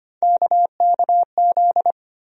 Morse code, heard as a keyed tone, about 25 wpm, 700 Hz. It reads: KKZ
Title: USA Call Sign Postfixes - 3 Letters 25wpm